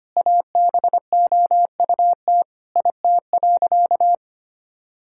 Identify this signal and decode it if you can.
Morse code, heard as a keyed tone, about 25 wpm, 700 Hz. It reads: ABOUT IT.
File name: practice / 25wpm / Sentences with 3-Word Phrases - 25wpm